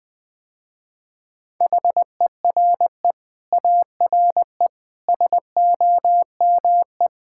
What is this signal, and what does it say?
Morse code, heard as a keyed tone, about 20 wpm, 700 Hz. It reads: HERE ARE SOME
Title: Sentences with 9-Word Phrases - Spoken Beginning - 20wpm